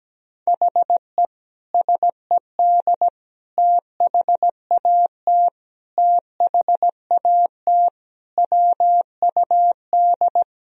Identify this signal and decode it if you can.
Morse code, heard as a keyed tone, about 17 wpm, 700 Hz. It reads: HE SED THAT THAT WUD
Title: Sentences from Top 200 Words and Abbreviations 17wpm